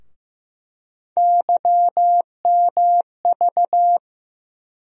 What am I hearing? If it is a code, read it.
Morse code, heard as a keyed tone, about 15 wpm, 700 Hz. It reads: YMV
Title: USA Call Sign Postfixes - 3 Letters 15wpm